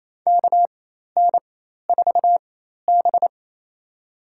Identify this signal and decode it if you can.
Morse code, heard as a keyed tone, about 28 wpm, spaced as for 13 wpm, 700 Hz. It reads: KN4B
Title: USA Call Signs - 4 Characters - 4x Character Spacing - 28wpm